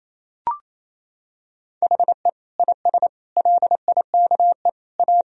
Morse code, heard as a keyed tone, about 28 wpm, 700 Hz. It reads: HE IS LIKE A
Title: Sentences from Top 1000 Words - Review Part 1 - 28wpm